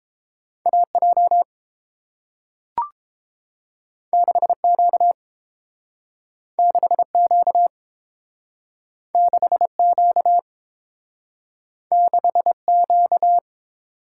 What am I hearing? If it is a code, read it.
Morse code, heard as a keyed tone, about 30 wpm, 700 Hz. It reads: AJ 6Q 6Q 6Q 6Q